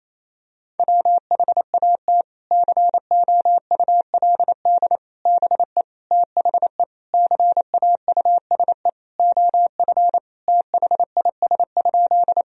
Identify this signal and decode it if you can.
Morse code, heard as a keyed tone, about 28 wpm, 700 Hz. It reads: WHAT COULD BE THE CAUSE OF THIS?
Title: Sentences from Top 1000 Words - Review Part 1 - 28wpm